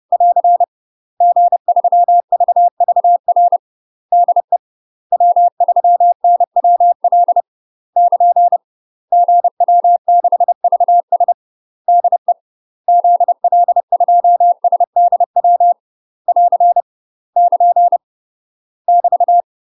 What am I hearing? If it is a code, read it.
Morse code, heard as a keyed tone, about 30 wpm, 700 Hz. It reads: <AR> G3VVR DE W3NWL <KN> GW6VS DE ZL2SDW <AR> <KN> <BT>
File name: UK MASTER QSOs TRAINING / 30WPM / UK MASTER QSOs 0018